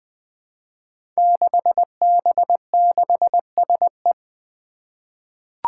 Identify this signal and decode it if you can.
Morse code, heard as a keyed tone, about 20 wpm, 700 Hz. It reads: 6B6SE
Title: Five Challenging Letter-Number Combinations 20wpm